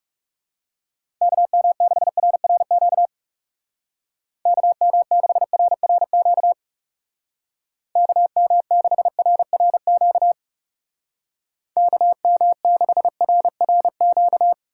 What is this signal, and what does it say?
Morse code, heard as a keyed tone, about 36 wpm, 700 Hz. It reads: KM6RRQ KM6RRQ KM6RRQ KM6RRQ
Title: Speed Racing - USA Call Signs 30wpm